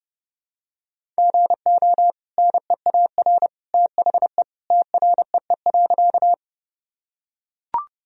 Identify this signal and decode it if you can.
Morse code, heard as a keyed tone, about 30 wpm, 700 Hz. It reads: GO NEAR THE TREE.
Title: Sentences from Top 1000 Words - Review Part 1 - 30wpm